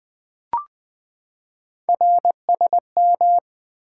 Morse code, heard as a keyed tone, about 20 wpm, 700 Hz. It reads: RSM
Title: Three Letter-Number Combinations - 20wpm